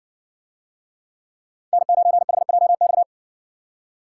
Morse code, heard as a keyed tone, about 60 wpm, 700 Hz. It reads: N9HJX